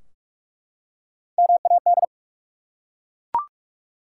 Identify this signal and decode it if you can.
Morse code, heard as a keyed tone, about 44 wpm, 700 Hz. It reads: MAD
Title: Three Letter Words - 45wpm